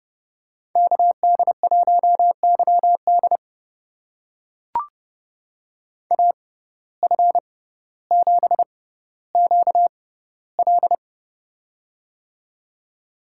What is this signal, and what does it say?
Morse code, heard as a keyed tone, about 28 wpm, spaced as for 15 wpm, 700 Hz. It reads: KD1YD AF7QL
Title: USA Call Signs - 5 Characters - 6x Character Spacing - 30wpm